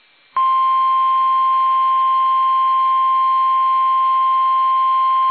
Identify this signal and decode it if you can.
Emergency Alert System (SAME) tones (the 1050 Hz attention tone). It is heard from 0.5 to 4.5 s.
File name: eas / KUSO / KUSO